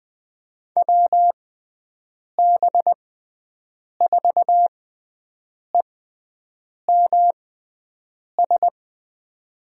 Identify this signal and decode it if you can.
Morse code, heard as a keyed tone, about 20 wpm, spaced as for 7 wpm, 700 Hz. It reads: WB4EMS
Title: USA Call Signs - 6x Character Spacing - 20wpm